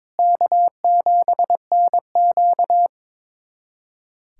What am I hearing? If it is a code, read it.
Morse code, heard as a keyed tone, about 22 wpm, 700 Hz. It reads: K7NQ
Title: USA Callsign DE Callsign - 4 Characters - 22wpm